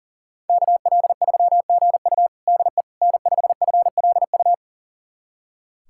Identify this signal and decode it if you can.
Morse code, heard as a keyed tone, about 40 wpm, 700 Hz. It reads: KL3GU DE N5FLU
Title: USA Callsign DE Callsign - 5 Characters - 40wpm